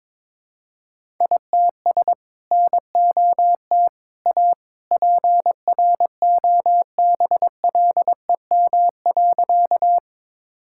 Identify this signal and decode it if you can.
Morse code, heard as a keyed tone, about 22 wpm, 700 Hz. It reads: ITS NOT A PROBLEM.
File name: practice / 22wpm / Sentences with 2-Word Phrases - 22wpm